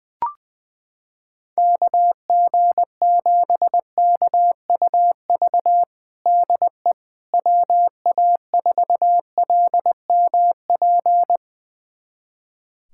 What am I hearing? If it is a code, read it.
Morse code, heard as a keyed tone, about 20 wpm, 700 Hz. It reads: KG7KUV DE WA4LMP
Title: USA Callsign DE Callsign - 6 Characters - 20wpm